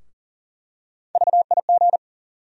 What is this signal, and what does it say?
Morse code, heard as a keyed tone, about 38 wpm, 700 Hz. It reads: VIG
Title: USA Call Sign Postfixes - 3 Letters 40wpm